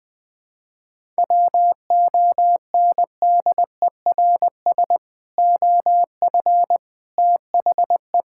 Morse code, heard as a keyed tone, about 20 wpm, 700 Hz. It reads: WONDERS OF THE
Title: Sentences with 3-Word Phrases - 20wpm